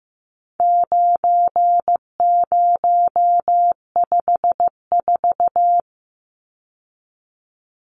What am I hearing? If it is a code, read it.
Morse code, heard as a keyed tone, about 15 wpm, 700 Hz. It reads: 9054
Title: Numbers - 4-Digit 15wpm